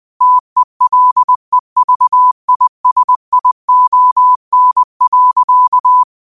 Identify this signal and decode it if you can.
Morse code, heard as a keyed tone, about 20 wpm, 1000 Hz. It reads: TELEVISION.